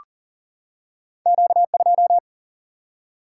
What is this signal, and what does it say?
Morse code, heard as a keyed tone, about 39 wpm, 700 Hz. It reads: Q 2
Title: Single-Letter Single-Number 40wpm